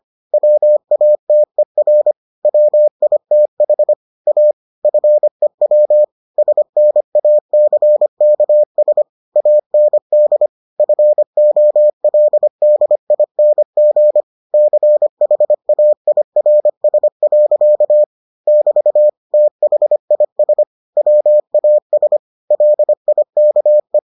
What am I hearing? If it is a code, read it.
Morse code, heard as a keyed tone, about 25 wpm, 600 Hz. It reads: WATER WITH A FEW SNACKS AND FOLDING CHAIRS. <BT> THIS WAS LIKE